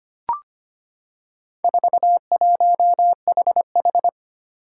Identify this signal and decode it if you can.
Morse code, heard as a keyed tone, about 25 wpm, 700 Hz. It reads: 41HH
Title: Four Challenging Letter-Number Combinations 25wpm